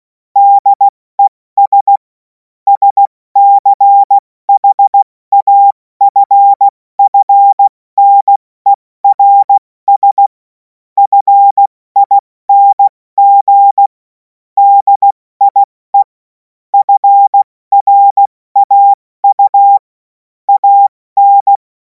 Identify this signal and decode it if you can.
Morse code, heard as a keyed tone, about 16 wpm, spaced as for 14 wpm, 800 Hz. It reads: DES SCHAFFNERS FING DIE FRAU AN
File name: Morsekurs / Witze / Witz 03. Februar 2026 70 F80